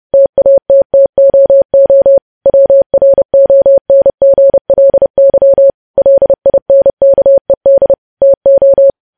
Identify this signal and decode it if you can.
Morse code, heard as a keyed tone, about 30 wpm, 550 Hz. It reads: TATTOO WRONGLY LINKED TO
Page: news headlines in morse code at 30 wpm